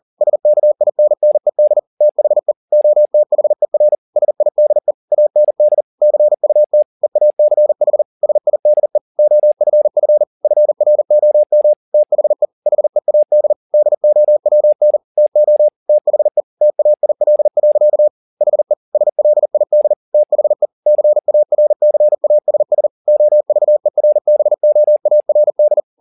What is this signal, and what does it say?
Morse code, heard as a keyed tone, about 40 wpm, 600 Hz. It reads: SKINNED THE OTHER SIDE AND CUT EACH SIDE OFF FROM THE HEAD DOWN TO THE TAIL. HE SLID THE CARCASS OVERBOARD
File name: the old man and the sea 2 40wpm